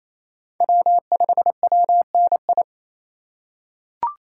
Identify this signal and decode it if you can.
Morse code, heard as a keyed tone, about 28 wpm, 700 Hz. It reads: W5WNI
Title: USA Call Signs - 5 Characters - 4x Character Spacing - 28wpm